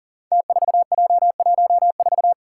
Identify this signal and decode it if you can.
Morse code, heard as a keyed tone, about 40 wpm, 700 Hz. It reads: T4J14
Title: Five Challenging Letter-Number Combinations 40wpm